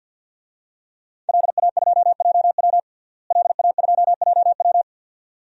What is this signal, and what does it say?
Morse code, heard as a keyed tone, about 47 wpm, 700 Hz. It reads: LA2JW LA2JW
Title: DXCC-100 Call Signs - 5 Characters - Repeated 2x - 50wpm